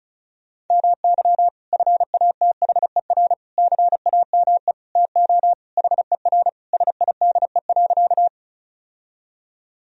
Morse code, heard as a keyed tone, about 36 wpm, 700 Hz. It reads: MY FATHER CAME TO HER SIDE.
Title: Sentences from Top 1000 Words - Review Part 1 - 35wpm